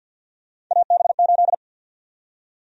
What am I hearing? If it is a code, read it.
Morse code, heard as a keyed tone, about 48 wpm, 700 Hz. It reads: AB8
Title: USA Call Sign Prefixes - Repeated 2x - 50wpm